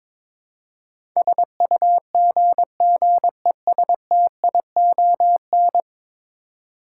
Morse code, heard as a keyed tone, about 22 wpm, 700 Hz. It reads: SUGGESTION
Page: morse-code-ninja-practice-22wpm